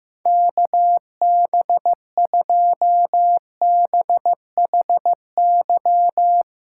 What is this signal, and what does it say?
Morse code, heard as a keyed tone, about 16 wpm, 700 Hz. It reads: KB2BHY
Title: USA Callsign DE Callsign - 6 Characters - 15wpm